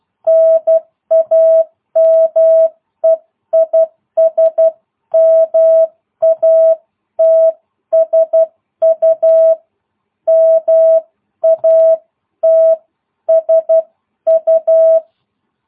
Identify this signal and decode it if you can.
Morse code, heard as a keyed tone, about 12 wpm, 660 Hz. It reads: NAME IS MATSU MATSU